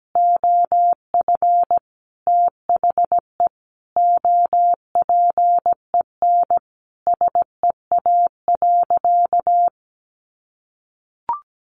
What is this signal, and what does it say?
Morse code, heard as a keyed tone, about 17 wpm, 700 Hz. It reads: OF THE OPEN SEA.
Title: Sentences from Top 1500 Words 17wpm